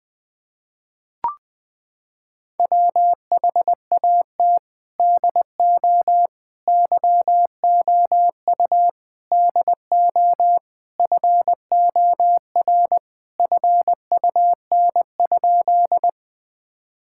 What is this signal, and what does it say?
Morse code, heard as a keyed tone, about 20 wpm, 700 Hz. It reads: WHAT DO YOU DO FOR FUN?
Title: Sentences with 4-Word Phrases - 20wpm